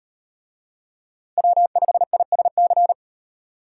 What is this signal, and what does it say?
Morse code, heard as a keyed tone, about 38 wpm, 700 Hz. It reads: W5ISC